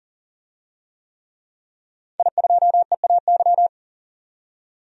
Morse code, heard as a keyed tone, about 40 wpm, 700 Hz. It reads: I2EAY
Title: DXCC-100 Call Signs - 5 Characters - 40wpm